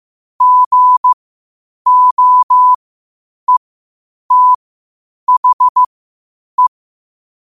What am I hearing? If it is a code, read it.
Morse code, heard as a keyed tone, about 15 wpm, spaced as for 8 wpm, 1000 Hz. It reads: GOETHE
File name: qotdfiles / 8wpm